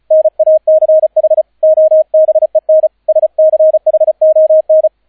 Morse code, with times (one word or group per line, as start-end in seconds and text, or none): 0.1-1.4 NACH
1.6-2.9 OBEN
3.1-4.9 SCHON